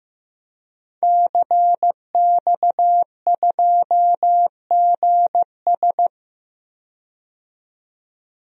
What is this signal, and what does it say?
Morse code, heard as a keyed tone, about 15 wpm, 700 Hz. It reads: CX2GS
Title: DXCC-100 Call Signs - 5 Characters - 15wpm